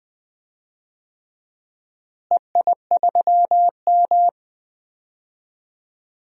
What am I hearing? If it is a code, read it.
Morse code, heard as a keyed tone, about 20 wpm, 700 Hz. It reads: EI3M
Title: DXCC-100 Call Signs - 4 Characters - Repeated 3x - 20wpm